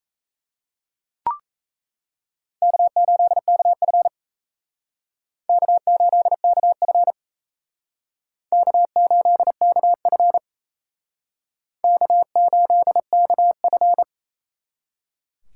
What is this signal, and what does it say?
Morse code, heard as a keyed tone, about 35 wpm, 700 Hz. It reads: K8KF K8KF K8KF K8KF